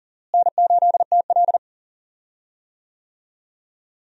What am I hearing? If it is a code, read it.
Morse code, heard as a keyed tone, about 37 wpm, 700 Hz. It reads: N8TL
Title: USA Call Signs - 4 Characters - Repeated 3x - 40wpm